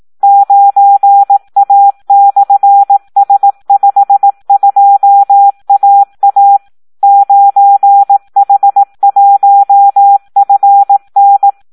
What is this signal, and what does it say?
Morse code, heard as a keyed tone, about 18 wpm, 800 Hz. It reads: 9A/S52AA 9H1FN
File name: Audio practica CW / SD 1 18